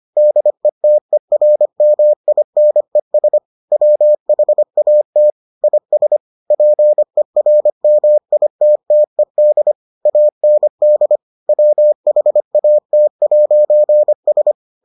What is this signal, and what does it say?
Morse code, heard as a keyed tone, about 25 wpm, 600 Hz. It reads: DETERMINES WHAT IS PERMITTED AND WHAT'S